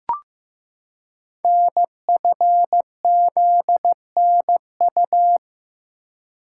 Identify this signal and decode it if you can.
Morse code, heard as a keyed tone, about 15 wpm, 700 Hz. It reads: NFZNU
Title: Code Groups - 5 Letters - 15wpm